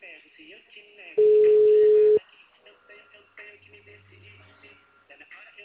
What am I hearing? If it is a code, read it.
Ringback tone (Europe, 425 Hz), with speech.
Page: Arapongas - PR